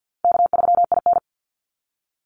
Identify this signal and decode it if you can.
Morse code, heard as a keyed tone, about 50 wpm, 700 Hz. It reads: K3IN
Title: Call Signs - 50wpm